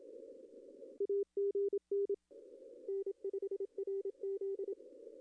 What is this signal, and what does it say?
Morse code, heard as a keyed tone, about 27 wpm, 400 Hz. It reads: AGN N5RZ
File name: contests / arrl-160-k5tr-op-w5tn -0600